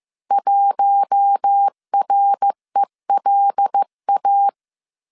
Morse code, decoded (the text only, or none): 1RELA